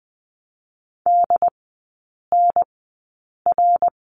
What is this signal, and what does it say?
Morse code, heard as a keyed tone, about 20 wpm, spaced as for 8 wpm, 700 Hz. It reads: DNR
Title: 067 Lesson 12e - Letter d - Three Characters 20wpm